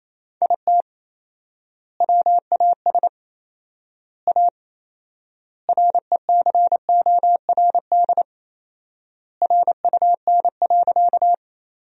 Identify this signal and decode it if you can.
Morse code, heard as a keyed tone, about 28 wpm, 700 Hz. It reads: IT WAS A RECORD RUN.
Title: 4x Word-Spacing - Sentences from Top 500 Words - 28wpm